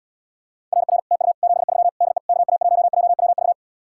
Morse code, heard as a keyed tone, about 79 wpm, 700 Hz. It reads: RR TU 73 DE KA1JWV